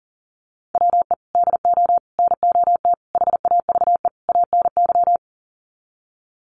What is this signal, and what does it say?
Morse code, heard as a keyed tone, about 40 wpm, 700 Hz. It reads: WE DO NOT HAVE ANY
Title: Sets of 5 Words 40wpm